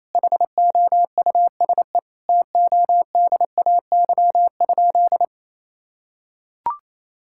Morse code, heard as a keyed tone, about 28 wpm, 700 Hz. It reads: HOUSE TODAY?
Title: Sentences from Top 1000 Words - Review Part 6 - 28wpm